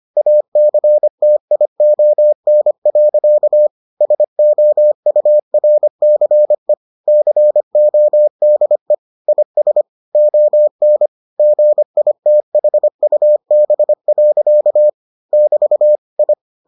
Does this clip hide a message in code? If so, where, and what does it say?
Yes: Morse code from 0.2 to 16.3 s, heard as a keyed tone, about 25 wpm, 600 Hz. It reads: ACTION. SOURCE CODE IS ON GITHUB. <BT> I